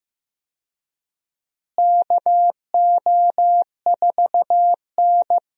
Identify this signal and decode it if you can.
Morse code, heard as a keyed tone, about 15 wpm, 700 Hz. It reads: KO4N
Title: USA Call Signs - 4 Characters - Repeated 3x - 15wpm